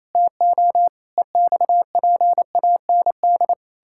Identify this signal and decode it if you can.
Morse code, heard as a keyed tone, about 27 wpm, 700 Hz. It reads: TO EXPAND
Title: Pangram Challenge v2 - 28wpm